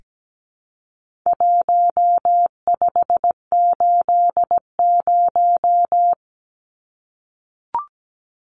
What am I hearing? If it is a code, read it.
Morse code, heard as a keyed tone, about 17 wpm, 700 Hz. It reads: 1580
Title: Numbers - Contest Style - 4-Digit 17wpm